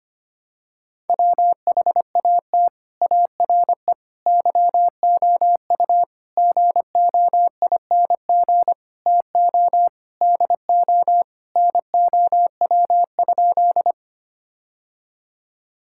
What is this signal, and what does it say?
Morse code, heard as a keyed tone, about 25 wpm, 700 Hz. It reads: WHAT ARE YOU GOING TO DO NOW?
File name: practice / 25wpm / Sentences with 3-Word Phrases - 25wpm